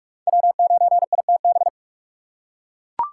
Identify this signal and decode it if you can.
Morse code, heard as a keyed tone, about 42 wpm, 700 Hz. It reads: W9ITB
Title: USA Call Signs - 5 Characters - 3x Character Spacing - 45wpm